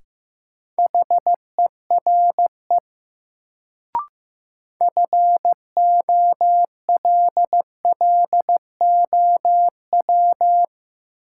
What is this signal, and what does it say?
Morse code, heard as a keyed tone, about 15 wpm, 700 Hz. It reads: HERE FOLLOW
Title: Sentences from Top 200 Words - New Word Drill - 15wpm